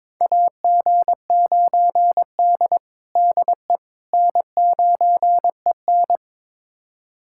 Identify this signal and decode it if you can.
Morse code, heard as a keyed tone, about 22 wpm, 700 Hz. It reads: AG9D DE N9EN
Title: USA Callsign DE Callsign - 4 Characters - 22wpm